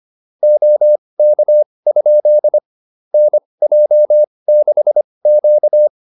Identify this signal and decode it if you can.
Morse code, heard as a keyed tone, about 25 wpm, spaced as for 20 wpm, 600 Hz. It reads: OK? NJ6Q